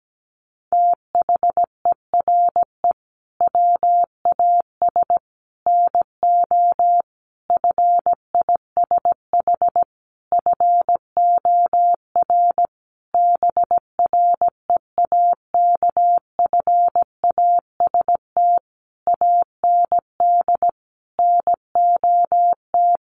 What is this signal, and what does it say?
Morse code, heard as a keyed tone, about 17 wpm, 700 Hz. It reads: THERE WAS NO FISH FOR BREAKFAST AND NOT
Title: Sentences from Top 1500 Words 17wpm